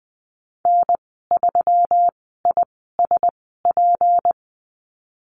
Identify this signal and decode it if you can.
Morse code, heard as a keyed tone, about 20 wpm, spaced as for 15 wpm, 700 Hz. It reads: N3ISP
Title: 173 Lesson 28g - Number 9 - Call Signs 20wpm